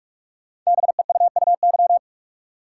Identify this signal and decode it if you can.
Morse code, heard as a keyed tone, about 44 wpm, 700 Hz. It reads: BEUUY
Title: Code Groups - 5 Letters - 45wpm